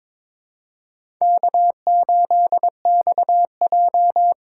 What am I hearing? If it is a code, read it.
Morse code, heard as a keyed tone, about 22 wpm, 700 Hz. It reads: K8XJ
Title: K1USN SST QSO Practice - 22wpm